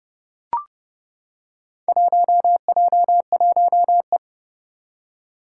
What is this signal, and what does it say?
Morse code, heard as a keyed tone, about 30 wpm, 700 Hz. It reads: 1J1E
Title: Four Challenging Letter-Number Combinations 30wpm